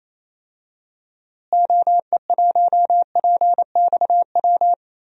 Morse code, heard as a keyed tone, about 27 wpm, 700 Hz. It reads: OE1PXW
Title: DXCC-100 Call Signs - 6 Characters - Repeated 2x - 28wpm